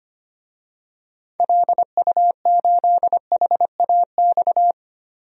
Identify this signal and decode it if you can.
Morse code, heard as a keyed tone, about 25 wpm, 700 Hz. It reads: LU8HAX
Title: DXCC-100 Call Signs - 6 Characters - 25wpm